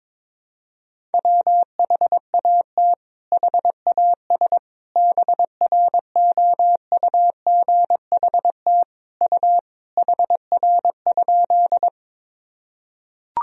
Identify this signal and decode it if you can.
Morse code, heard as a keyed tone, about 22 wpm, 700 Hz. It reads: WHAT HAS BROUGHT U HR?
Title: Sentences from Top 500 Words and Abbreviations 22wpm